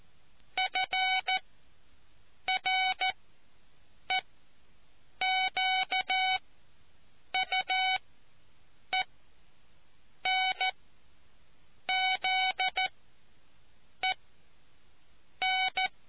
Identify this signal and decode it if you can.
Morse code, heard as a keyed tone, about 14 wpm, spaced as for 6 wpm, 750 Hz. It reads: FREQUENZEN